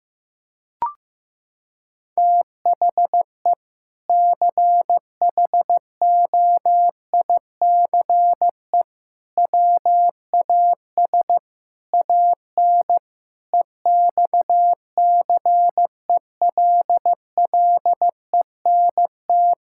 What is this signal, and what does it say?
Morse code, heard as a keyed tone, about 15 wpm, 700 Hz. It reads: THE CHOICE WAS AN EXCELLENT
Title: Sentences from Top 1000 Words - Review Part 6 - 15wpm